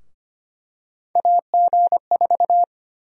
Morse code, heard as a keyed tone, about 25 wpm, 700 Hz. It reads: AG4
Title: USA Call Sign Prefixes - 25wpm